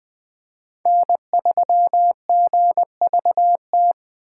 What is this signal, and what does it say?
Morse code, heard as a keyed tone, about 20 wpm, 700 Hz. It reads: N3GVT
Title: USA Call Signs - 5 Characters - 4x Character Spacing - 20wpm